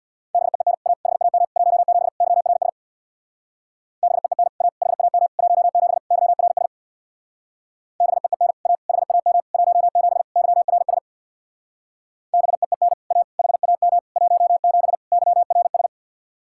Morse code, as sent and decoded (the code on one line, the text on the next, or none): -... . -. / .- / .... .- -- / .---- --... / -.-- .-. ... / -... . -. / .- / .... .- -- / .---- --... / -.-- .-. ... / -... . -. / .- / .... .- -- / .---- --... / -.-- .-. ... / -... . -. / .- / .... .- -- / .---- --... / -.-- .-. ...
BEEN A HAM 17 YRS BEEN A HAM 17 YRS BEEN A HAM 17 YRS BEEN A HAM 17 YRS